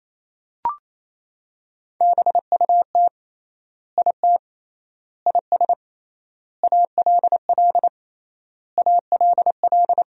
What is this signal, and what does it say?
Morse code, heard as a keyed tone, about 28 wpm, 700 Hz. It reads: BUT IT IS ALL ALL
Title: 3x Word-Spacing - Sentences from Top 300 Words - 28wpm